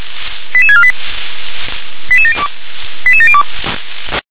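Five-tone selective call (ZVEI), heard as quick runs of five tones, tones of about 70 ms. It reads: 88648 88702 88702
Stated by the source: ZVEI five-tone selective call 88742 received by SDR.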